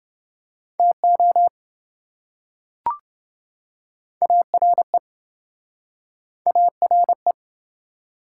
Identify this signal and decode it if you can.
Morse code, heard as a keyed tone, about 28 wpm, 700 Hz. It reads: TO ARE ARE